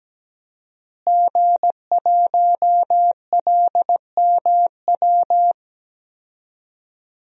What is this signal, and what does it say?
Morse code, heard as a keyed tone, about 17 wpm, 700 Hz. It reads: G1LMW
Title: DXCC-100 Call Signs - 5 Characters - Repeated 2x - 17wpm